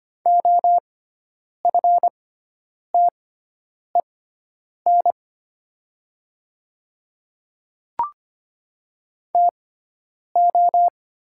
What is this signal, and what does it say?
Morse code, heard as a keyed tone, about 25 wpm, spaced as for 9 wpm, 700 Hz. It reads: OFTEN TO
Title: 6x Character Spacing - Top 300 Words - 25wpm